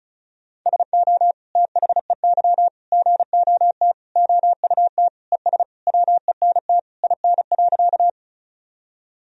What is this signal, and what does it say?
Morse code, heard as a keyed tone, about 34 wpm, 700 Hz. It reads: SO THEY GOT OUT ES WENT IN.
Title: Sentences from Top 300 Words and Abbreviations 35wpm